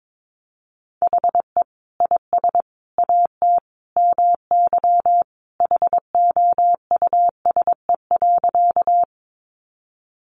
Morse code, heard as a keyed tone, about 22 wpm, 700 Hz. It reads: HE IS AT MY HOUSE.
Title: Sentences from Top 200 Words 22wpm